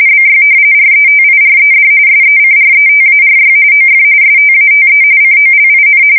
RTTY, about 45.45 baud. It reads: BY RETIRED SECRETARY OF STATE COLIN
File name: fsk170 / battleshipnj